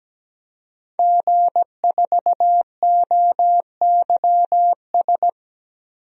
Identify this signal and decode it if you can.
Morse code, heard as a keyed tone, about 17 wpm, 700 Hz. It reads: G4OYS
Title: DXCC-100 Call Signs - 5 Characters - Repeated 2x - 17wpm